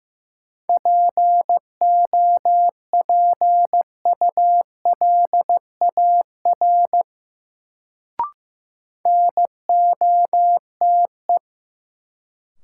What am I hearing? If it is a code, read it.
Morse code, heard as a keyed tone, about 15 wpm, 700 Hz. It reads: POPULAR NOTE
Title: Sets of 5 Words - New Word Drill - 15wpm